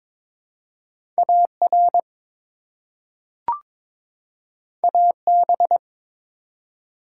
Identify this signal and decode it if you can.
Morse code, heard as a keyed tone, about 22 wpm, 700 Hz. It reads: AR AB
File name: ARRL Sections v2 - 22wpm